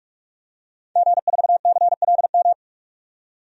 Morse code, heard as a keyed tone, about 46 wpm, 690 Hz. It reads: G4CLM
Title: DXCC-100 Call Signs - 5 Characters - 45wpm